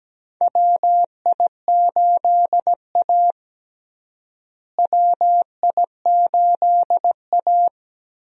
Morse code, heard as a keyed tone, about 17 wpm, 700 Hz. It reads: WI8A WI8A
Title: USA Call Signs - 4 Characters - Repeated 3x - 17wpm